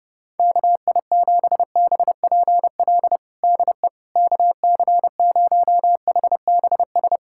Morse code, heard as a keyed tone, about 32 wpm, 700 Hz. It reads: KI7BPL DE KC0HBS
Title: USA Callsign DE Callsign - 6 Characters - 30wpm